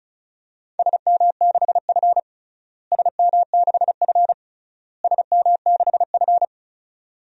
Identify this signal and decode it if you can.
Morse code, heard as a keyed tone, about 36 wpm, 700 Hz. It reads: SM6F SM6F SM6F
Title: DXCC-100 Call Signs - 4 Characters - Repeated 3x - 35wpm